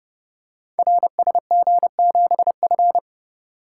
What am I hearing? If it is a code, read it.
Morse code, heard as a keyed tone, about 30 wpm, 700 Hz. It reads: RSG7F
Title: Code Groups - 5 Letter:Numbers - 30wpm